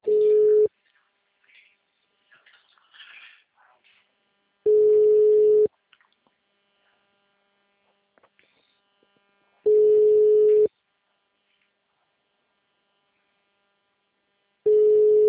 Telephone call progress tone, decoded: ringback tone

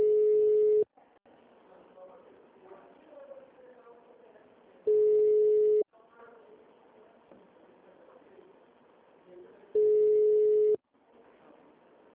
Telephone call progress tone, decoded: ringback tone

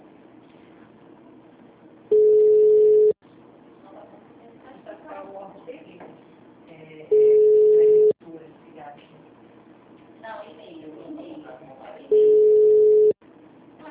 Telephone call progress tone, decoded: ringback tone

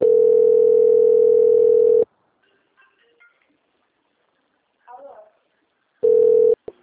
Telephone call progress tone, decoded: ringback tone

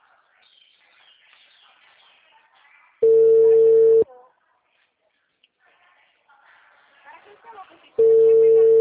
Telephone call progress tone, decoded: ringback tone

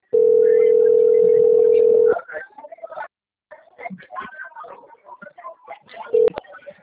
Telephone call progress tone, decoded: ringback tone

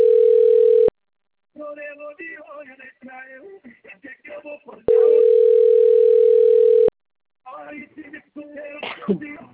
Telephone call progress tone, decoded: ringback tone

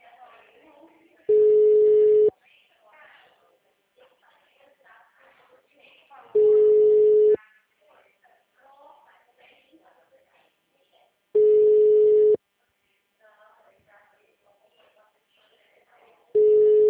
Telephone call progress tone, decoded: ringback tone

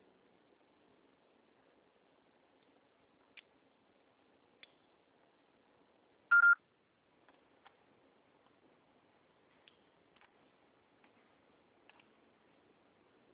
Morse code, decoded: none

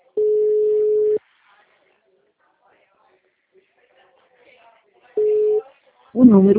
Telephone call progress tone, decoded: ringback tone

